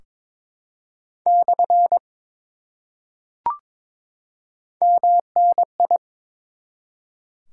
Morse code, heard as a keyed tone, about 22 wpm, 700 Hz. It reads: / MNI
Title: Top 90 QSO Elements 22wpm